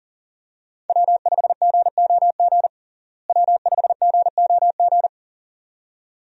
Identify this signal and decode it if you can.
Morse code, heard as a keyed tone, about 40 wpm, 700 Hz. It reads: W5GOG W5GOG
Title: USA Call Signs - 5 Characters - Repeated 2x - 40wpm